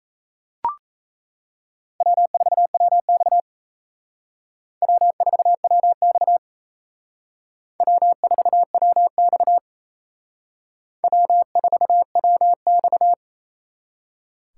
Morse code, heard as a keyed tone, about 35 wpm, 700 Hz. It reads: W4WX W4WX W4WX W4WX